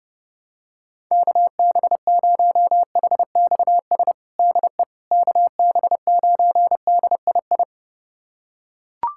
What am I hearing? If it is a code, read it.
Morse code, heard as a keyed tone, about 30 wpm, 700 Hz. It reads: KB0HXS DE KB9DII
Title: USA Callsign DE Callsign - 6 Characters - 30wpm